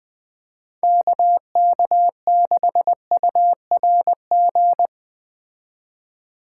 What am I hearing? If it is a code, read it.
Morse code, heard as a keyed tone, about 21 wpm, 700 Hz. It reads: KK6URG
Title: Speed Racing - USA 2x3 Call Signs 17wpm